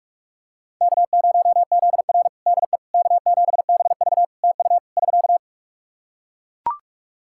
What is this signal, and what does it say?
Morse code, heard as a keyed tone, about 42 wpm, 700 Hz. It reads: K0ZR DE K7BV TU <SK>
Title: Practice QSO - Exchange 4 - Ending - 45wpm